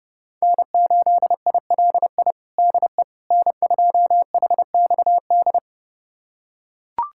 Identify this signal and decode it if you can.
Morse code, heard as a keyed tone, about 30 wpm, 700 Hz. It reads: N8ILI DE N2HXD
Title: USA Callsign DE Callsign - 5 Characters - 30wpm